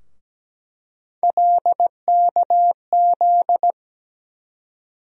Morse code, heard as a keyed tone, about 17 wpm, 700 Hz. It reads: LKZ
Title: USA Call Sign Postfixes - 3 Letters 17wpm